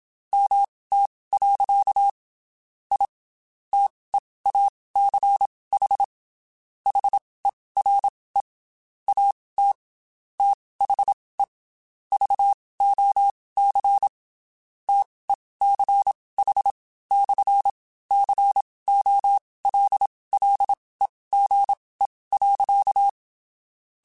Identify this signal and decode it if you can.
Morse code, heard as a keyed tone, about 27 wpm, spaced as for 19 wpm, 800 Hz. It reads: MT. I TEACH HERE AT THE VOC TECH/COLLEGE.
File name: QSOs / QSOs-mp3s / WD0BGP 1@25